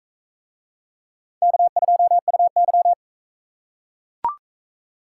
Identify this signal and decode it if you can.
Morse code, heard as a keyed tone, about 38 wpm, 700 Hz. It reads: K2UY